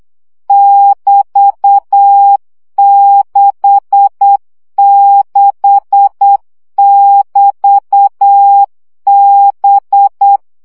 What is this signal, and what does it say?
Morse code, heard as a keyed tone, about 8 wpm, 800 Hz. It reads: <BT>66<BT>B
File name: CW gyakorlas b 6 egyenlo minusz 7 baud